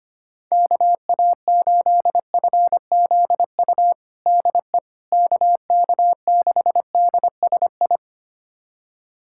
Morse code, heard as a keyed tone, about 25 wpm, 700 Hz. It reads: KA8FZU DE KK6DSI
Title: USA Callsign DE Callsign - 6 Characters - 25wpm